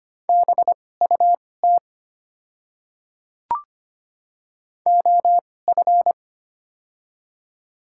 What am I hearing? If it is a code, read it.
Morse code, heard as a keyed tone, about 25 wpm, spaced as for 18 wpm, 700 Hz. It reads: BUT OF
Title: 2x Character Spacing - Top 100 Words - 25wpm